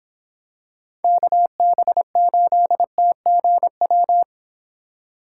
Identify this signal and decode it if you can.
Morse code, heard as a keyed tone, about 26 wpm, 700 Hz. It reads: KB8TGW